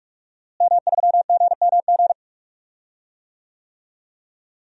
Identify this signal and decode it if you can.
Morse code, heard as a keyed tone, about 46 wpm, 700 Hz. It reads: M3GMG
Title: DXCC-100 Call Signs - 5 Characters - Repeated 3x - 45wpm